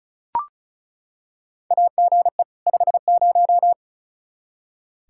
Morse code, heard as a keyed tone, about 36 wpm, 700 Hz. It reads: AGE 50
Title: Practice QSO - Exchange 3 - More Info - 35wpm